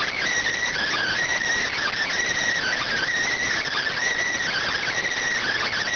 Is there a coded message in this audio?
Yes: Emergency Alert System (SAME) tones (header bursts) from 0.6 to 5.0 s, heard as harsh buzzing bursts.